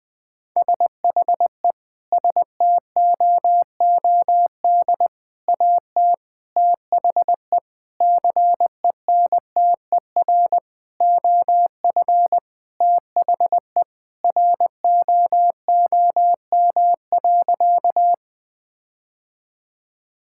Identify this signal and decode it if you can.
Morse code, heard as a keyed tone, about 20 wpm, 700 Hz. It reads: SHE STOOD AT THE CENTER OF THE ROOM.
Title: Sentences with 5-Word Phrases - Spoken Beginning - 20wpm